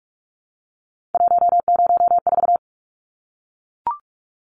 Morse code, heard as a keyed tone, about 44 wpm, 700 Hz. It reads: 104
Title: Numbers - 3-Digit 45wpm